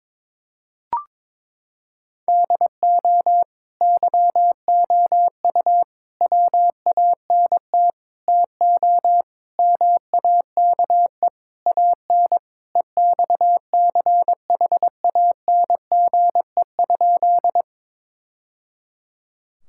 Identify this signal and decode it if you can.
Morse code, heard as a keyed tone, about 21 wpm, 700 Hz. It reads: DO YOU WANT TO MAKE AN EXCHANGE?
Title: Sentences from Top 1000 Words - Review Part 6 - 22wpm